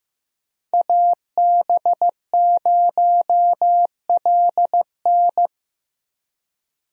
Morse code, heard as a keyed tone, about 15 wpm, 700 Hz. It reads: AB0LN
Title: USA Call Signs - 5 Characters - 15wpm